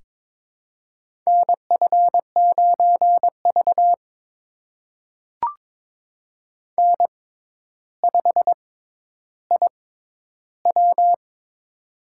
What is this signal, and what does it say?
Morse code, heard as a keyed tone, about 22 wpm, spaced as for 11 wpm, 700 Hz. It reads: NF9V N5IW
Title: USA Call Signs - 4 Characters - 6x Character Spacing - 22wpm